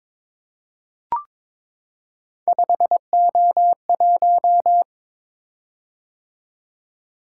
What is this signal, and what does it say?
Morse code, heard as a keyed tone, about 22 wpm, 700 Hz. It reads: 5O1
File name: practice / 22wpm / Three Letter-Number Combinations - 22wpm